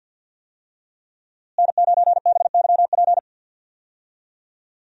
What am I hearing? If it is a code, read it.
Morse code, heard as a keyed tone, about 52 wpm, 700 Hz. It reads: N9BYP
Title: USA Call Signs - 5 Characters - Repeated 2x - 50wpm